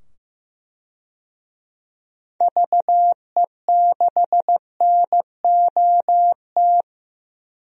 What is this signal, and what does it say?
Morse code, heard as a keyed tone, about 15 wpm, 700 Hz. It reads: VE6NOT
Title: DXCC-100 Call Signs - 6 Characters - 15wpm